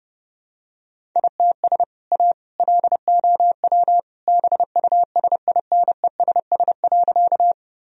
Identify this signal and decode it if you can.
Morse code, heard as a keyed tone, about 32 wpm, 700 Hz. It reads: ITS A LOW BUSINESS.
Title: Sentences from Top 1000 Words - Review Part 1 - 30wpm